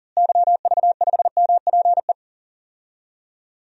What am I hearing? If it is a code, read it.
Morse code, heard as a keyed tone, about 40 wpm, 700 Hz. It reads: YV5MPE
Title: DXCC-100 Call Signs - 6 Characters - Repeated 3x - 40wpm